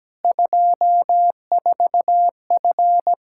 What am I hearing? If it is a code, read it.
Morse code, heard as a keyed tone, about 17 wpm, 700 Hz. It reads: 24F
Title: Practice QSO - Exchange 2 - Info - 17wpm